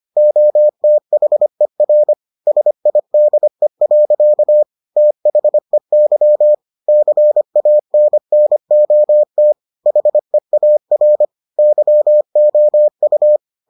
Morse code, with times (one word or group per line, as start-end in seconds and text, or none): 0.2-2.1 OTHER
2.5-4.6 SIDE.
5.0-6.5 THEY
6.9-9.5 CANNOT
9.9-11.2 HEAR
11.6-13.4 YOU